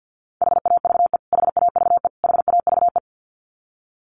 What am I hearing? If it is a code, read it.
Morse code, heard as a keyed tone, about 50 wpm, 700 Hz. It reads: HAVE HAVE HAVE
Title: Top 200 Words Repeated 50wpm